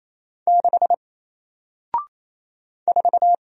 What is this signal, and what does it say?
Morse code, heard as a keyed tone, about 28 wpm, 700 Hz. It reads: 6 4
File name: Single-Letter-Number-28wpm